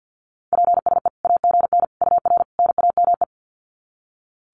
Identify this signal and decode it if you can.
Morse code, heard as a keyed tone, about 50 wpm, 700 Hz. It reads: PSE AGN UR NAME
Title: Common QSO Phrases 50wpm